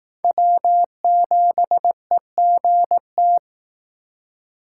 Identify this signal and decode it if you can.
Morse code, heard as a keyed tone, about 18 wpm, 700 Hz. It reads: W7EGT
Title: Speed Racing - USA 1x3 Call Signs 15wpm